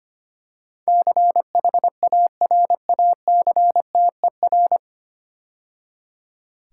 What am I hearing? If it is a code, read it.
Morse code, heard as a keyed tone, about 25 wpm, 700 Hz. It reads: CHARACTER
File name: practice / 25wpm / 9 Letter Words - First 3 Letters Given - 25wpm